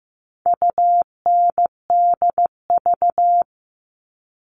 Morse code, heard as a keyed tone, about 15 wpm, 700 Hz. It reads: UNDV
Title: All Cut Numbers - 4-Digit 15wpm